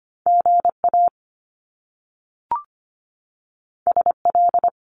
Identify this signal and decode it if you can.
Morse code, heard as a keyed tone, about 25 wpm, 700 Hz. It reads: GA SL
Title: Top Two Letter Combinations in Words 25wpm